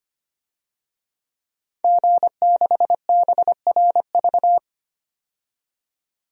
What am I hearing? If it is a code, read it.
Morse code, heard as a keyed tone, about 25 wpm, 700 Hz. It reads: G6BRV
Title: DXCC-100 Call Signs - 5 Characters - 25wpm